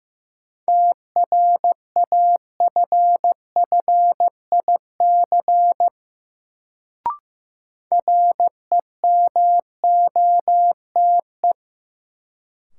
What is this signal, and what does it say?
Morse code, heard as a keyed tone, about 15 wpm, 700 Hz. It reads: TRAFFIC REMOTE
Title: Sentences from Top 1300 Words - New Word Drill Alt - 15wpm